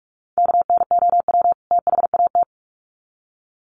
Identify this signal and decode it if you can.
Morse code, heard as a keyed tone, about 44 wpm, 700 Hz. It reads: KNOW THAT
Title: Sets of 2 Words 45wpm